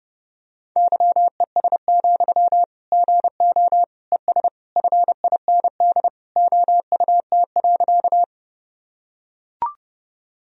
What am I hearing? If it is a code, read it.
Morse code, heard as a keyed tone, about 30 wpm, 700 Hz. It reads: YES, GO ES FIND OUT.
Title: Sentences from Top 500 Words and Abbreviations 30wpm